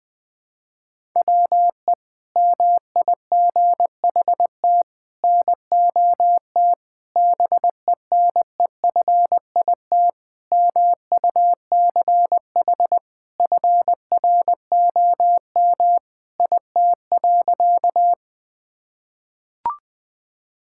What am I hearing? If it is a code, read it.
Morse code, heard as a keyed tone, about 21 wpm, 700 Hz. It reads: WE MIGHT NOT BENEFIT MUCH FROM IT.
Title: Sentences from Top 1000 Words - Review Part 6 - 20wpm